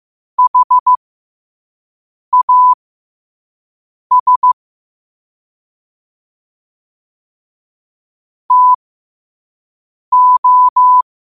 Morse code, heard as a keyed tone, about 15 wpm, spaced as for 5 wpm, 1000 Hz. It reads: HAS TO
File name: 2026-03-16-5wpm